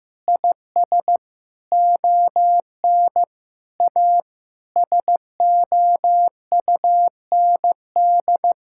Morse code, heard as a keyed tone, about 15 wpm, 700 Hz. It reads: IS ON A SOUND
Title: Sentences from Top 1000 Words - Review Part 5 - 15wpm